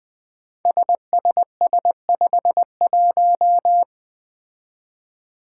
Morse code, heard as a keyed tone, about 20 wpm, 700 Hz. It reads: SSS51
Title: Five Challenging Letter-Number Combinations 20wpm